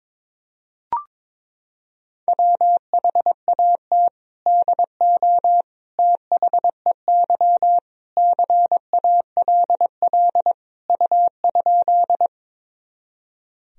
Morse code, heard as a keyed tone, about 22 wpm, 700 Hz. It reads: WHAT DO THEY CALL U?
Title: Sentences from Top 100 Words and Abbreviations 22wpm